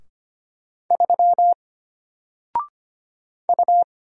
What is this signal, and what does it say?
Morse code, heard as a keyed tone, about 25 wpm, 700 Hz. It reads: 3 U
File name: Single-Letter-Number-25wpm